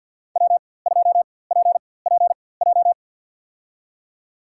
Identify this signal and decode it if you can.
Morse code, heard as a keyed tone, about 51 wpm, spaced as for 23 wpm, 700 Hz. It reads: W2PPJ
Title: USA Call Signs - 5 Characters - 4x Character Spacing - 50wpm